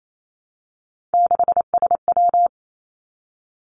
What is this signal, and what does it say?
Morse code, heard as a keyed tone, about 28 wpm, 700 Hz. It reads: 6SW
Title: Three Challenging Letter-Number Combinations 28wpm